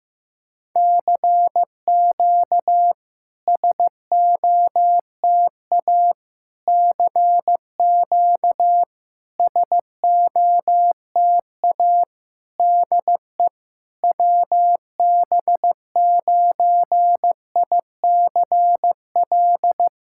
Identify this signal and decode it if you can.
Morse code, heard as a keyed tone, about 14 wpm, 700 Hz. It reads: CQ SOTA CQ SOTA DE WB9ICL
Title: SOTA QSO Practice 15wpm